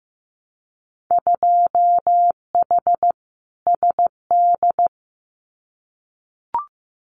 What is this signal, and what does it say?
Morse code, heard as a keyed tone, about 15 wpm, 700 Hz. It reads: 2H SD
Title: Winter Field Day Exchanges 15wpm